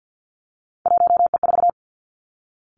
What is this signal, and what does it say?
Morse code, heard as a keyed tone, about 50 wpm, 700 Hz. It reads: 1E4
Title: Three Challenging Letter-Number Combinations 50wpm